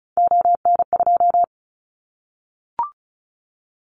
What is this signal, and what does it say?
Morse code, heard as a keyed tone, about 34 wpm, 700 Hz. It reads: ON2
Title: Callsign Prefixes for Top 25 DXCC - 35wpm